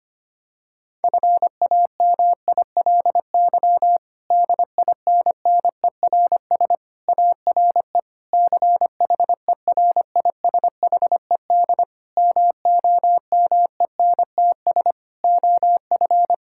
Morse code, heard as a keyed tone, about 25 wpm, 700 Hz. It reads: FAMILY DINNERS ARE CHERISHED MOMENTS OF
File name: practice / 25wpm / Difficult Sentences with Hints - 25wpm